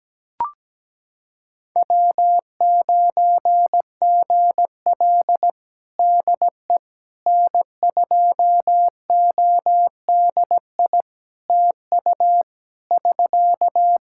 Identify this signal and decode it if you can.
Morse code, heard as a keyed tone, about 17 wpm, 700 Hz. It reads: W9GL DE N2ODI TU <SK>
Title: Practice QSO - Exchange 4 - Ending - 17wpm